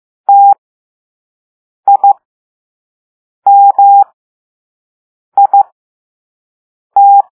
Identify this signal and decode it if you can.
Morse code, heard as a keyed tone, about 15 wpm, spaced as for 5 wpm, 800 Hz. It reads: TIMIT
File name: Morse / Level2-All-A